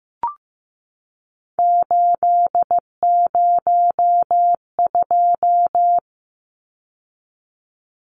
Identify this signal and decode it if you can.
Morse code, heard as a keyed tone, about 15 wpm, 700 Hz. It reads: 802
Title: Numbers - 3-Digit 15wpm